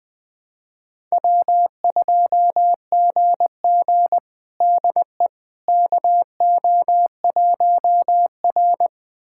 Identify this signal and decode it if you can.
Morse code, heard as a keyed tone, about 20 wpm, 700 Hz. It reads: W2GG DE KO1R
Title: USA Callsign DE Callsign - 4 Characters - 20wpm